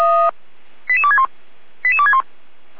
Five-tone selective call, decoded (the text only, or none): 88261 88261